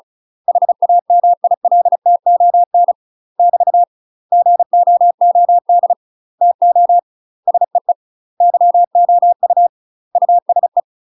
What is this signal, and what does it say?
Morse code, heard as a keyed tone, about 36 wpm, 700 Hz. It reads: HAMILTON <BT> GOOD TO SEE YOU USE
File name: UK MASTER QSOs TRAINING / 35WPM / UK MASTER QSOs 0026